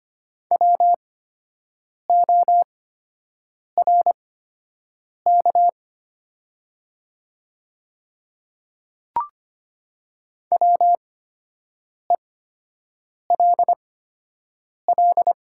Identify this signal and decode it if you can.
Morse code, heard as a keyed tone, about 25 wpm, spaced as for 7 wpm, 700 Hz. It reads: WORK WELL
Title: 8x Character Spacing - Top 200 Words - 25wpm